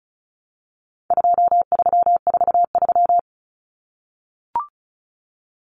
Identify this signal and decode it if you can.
Morse code, heard as a keyed tone, about 34 wpm, 700 Hz. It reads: 2343
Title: Numbers - 4-Digit 35wpm